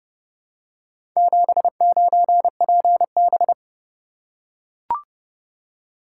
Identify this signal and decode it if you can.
Morse code, heard as a keyed tone, about 30 wpm, 700 Hz. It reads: 79PB